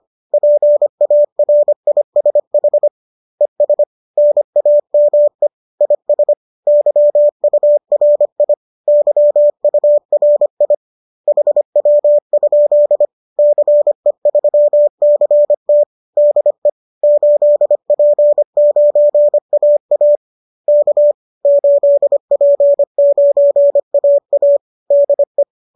Morse code, heard as a keyed tone, about 25 wpm, 590 Hz. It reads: PARISH ES NAME IS YURI YURI HW? CE3CT DE 8P9AA K 8P9AA DE